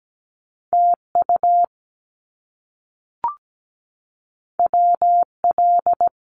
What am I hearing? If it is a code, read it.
Morse code, heard as a keyed tone, about 17 wpm, 700 Hz. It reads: TU WL
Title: Abbreviations Repeated 3x - 17wpm